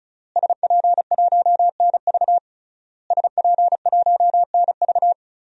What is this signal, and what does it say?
Morse code, heard as a keyed tone, about 36 wpm, 700 Hz. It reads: SP1NV SP1NV
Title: DXCC-100 Call Signs - 5 Characters - Repeated 3x - 35wpm